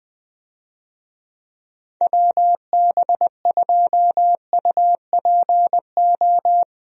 Morse code, heard as a keyed tone, about 20 wpm, 700 Hz. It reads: WB2UPO
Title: USA Call Signs - 6 Characters - Repeated 3x - 20wpm